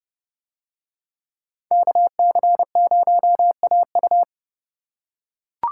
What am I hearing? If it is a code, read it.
Morse code, heard as a keyed tone, about 29 wpm, 700 Hz. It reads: KC0AU